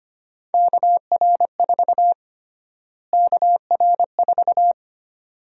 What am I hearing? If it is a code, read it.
Morse code, heard as a keyed tone, about 25 wpm, 700 Hz. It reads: KR4 KR4
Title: USA Call Sign Prefixes - Repeated 2x - 25wpm